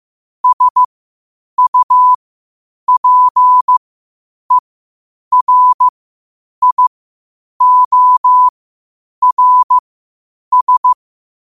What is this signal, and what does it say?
Morse code, heard as a keyed tone, about 15 wpm, spaced as for 8 wpm, 1000 Hz. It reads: SUPERIORS
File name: qotdfiles / 8wpm